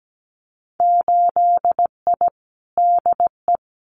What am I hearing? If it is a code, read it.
Morse code, heard as a keyed tone, about 17 wpm, 700 Hz. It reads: 8I DE
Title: Winter Field Day Exchanges Encore 17wpm